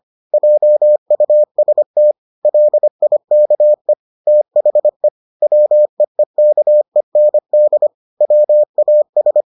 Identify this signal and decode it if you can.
Morse code, heard as a keyed tone, about 25 wpm, 600 Hz. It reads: JUST LIKE THE WEEKEND WAS